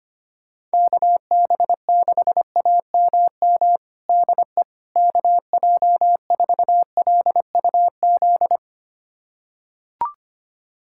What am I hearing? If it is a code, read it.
Morse code, heard as a keyed tone, about 25 wpm, 700 Hz. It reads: KB6AMM DE KJ4LUZ
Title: USA Callsign DE Callsign - 6 Characters - 25wpm